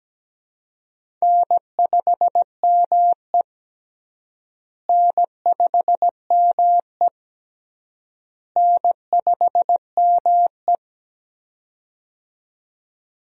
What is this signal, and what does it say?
Morse code, heard as a keyed tone, about 17 wpm, 700 Hz. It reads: N5ME N5ME N5ME
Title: USA Call Signs - 4 Characters - Repeated 3x - 17wpm